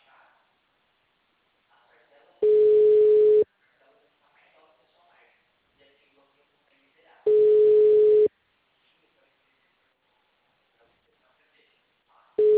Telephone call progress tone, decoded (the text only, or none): ringback tone